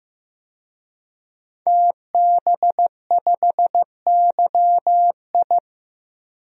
Morse code, heard as a keyed tone, about 15 wpm, 700 Hz. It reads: TB5YI
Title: DXCC-100 Call Signs - 5 Characters - 15wpm